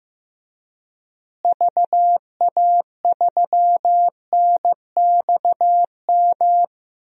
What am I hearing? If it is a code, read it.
Morse code, heard as a keyed tone, about 15 wpm, 700 Hz. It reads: VA3NXM
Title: DXCC-100 Call Signs - 6 Characters - Repeated 2x - 15wpm